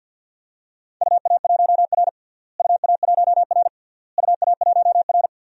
Morse code, heard as a keyed tone, about 51 wpm, 700 Hz. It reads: UA1R UA1R UA1R
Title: DXCC-100 Call Signs - 4 Characters - Repeated 3x - 50wpm